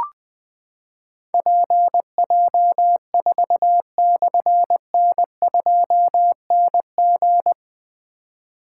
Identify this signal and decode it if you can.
Morse code, heard as a keyed tone, about 20 wpm, 700 Hz. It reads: PJ4/N2NG
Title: Difficult Call Signs 20wpm